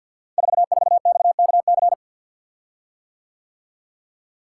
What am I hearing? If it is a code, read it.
Morse code, heard as a keyed tone, about 51 wpm, 700 Hz. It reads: 44XKC
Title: Code Groups - 5 Letter:Numbers - 50wpm